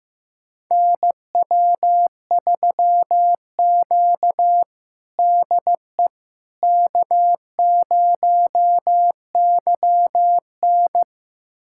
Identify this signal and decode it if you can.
Morse code, heard as a keyed tone, about 15 wpm, 700 Hz. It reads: NW3Q DE K0YN
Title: USA Callsign DE Callsign - 4 Characters - 15wpm